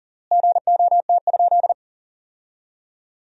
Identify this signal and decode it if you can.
Morse code, heard as a keyed tone, about 37 wpm, 700 Hz. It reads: GOT?
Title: 3x Word-Spacing - Sentences from Top 300 Words - 40wpm